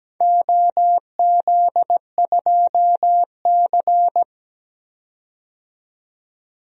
Morse code, heard as a keyed tone, about 17 wpm, 700 Hz. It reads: OZ2C
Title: DXCC-100 Call Signs - 4 Characters - 17wpm